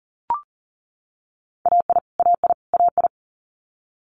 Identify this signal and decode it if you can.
Morse code, heard as a keyed tone, about 38 wpm, 690 Hz. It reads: AI AI AI
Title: Multi-letter Phonemes Repeated 3x 40wpm